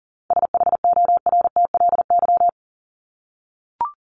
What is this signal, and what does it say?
Morse code, heard as a keyed tone, about 40 wpm, 700 Hz. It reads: SHORTLY
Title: LY Suffix - Top 100 Words with Base Words 40wpm